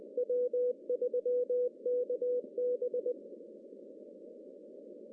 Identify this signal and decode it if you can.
Morse code, heard as a keyed tone, about 20 wpm, 490 Hz. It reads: W3KB
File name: CW / cqww-cw-k5tr